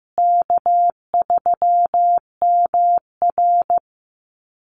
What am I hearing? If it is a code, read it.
Morse code, heard as a keyed tone, about 15 wpm, 700 Hz. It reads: K3MR
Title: Call Signs Repeated 3x - 15wpm